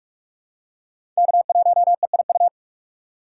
Morse code, heard as a keyed tone, about 43 wpm, 700 Hz. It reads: K1EIU
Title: USA Call Signs - 5 Characters - 6x Character Spacing - 45wpm